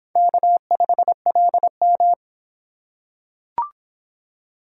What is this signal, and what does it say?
Morse code, heard as a keyed tone, about 24 wpm, 700 Hz. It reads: K5LM